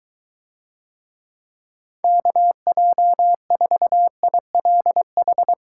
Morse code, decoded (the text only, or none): KJ4ILH